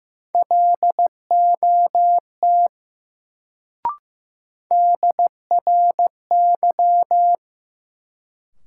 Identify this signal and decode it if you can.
Morse code, heard as a keyed tone, about 15 wpm, 700 Hz. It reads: LOT DRY
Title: Sentences from Top 500 Words - New Word Drill - 15wpm